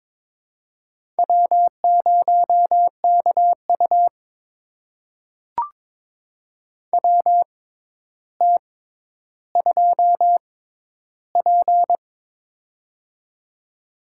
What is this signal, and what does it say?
Morse code, heard as a keyed tone, about 21 wpm, spaced as for 11 wpm, 700 Hz. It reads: W0KU WT2P
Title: USA Call Signs - 4 Characters - 6x Character Spacing - 22wpm